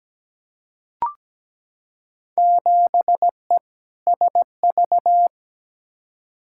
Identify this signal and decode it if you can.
Morse code, heard as a keyed tone, about 17 wpm, 700 Hz. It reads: 7E SV
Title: Field Day Exchanges Encore - 17wpm